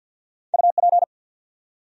Morse code, heard as a keyed tone, about 53 wpm, 700 Hz. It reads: UP
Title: Top 100 Words Repeated 50wpm